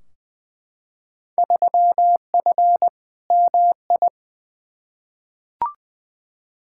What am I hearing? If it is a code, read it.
Morse code, heard as a keyed tone, about 20 wpm, 700 Hz. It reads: 3F MI
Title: Field Day Exchanges - 20wpm